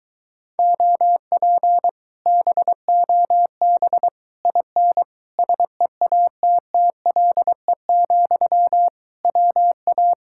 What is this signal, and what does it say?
Morse code, heard as a keyed tone, about 23 wpm, 700 Hz. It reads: OP BOB IN SEATTLE, WA